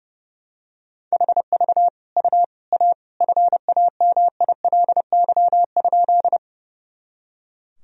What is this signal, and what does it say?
Morse code, heard as a keyed tone, about 30 wpm, 700 Hz. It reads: HV U A FAMILY?
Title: Sentences from Top 400 Words and Abbreviations 30wpm